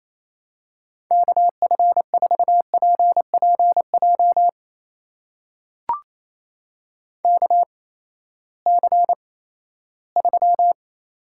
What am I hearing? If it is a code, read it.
Morse code, heard as a keyed tone, about 28 wpm, spaced as for 16 wpm, 700 Hz. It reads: KF4PPJ KC3
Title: USA Call Signs - 6 Characters - 8x Character Spacing - 28wpm